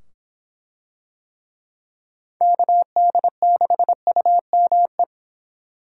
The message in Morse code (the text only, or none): KD6UME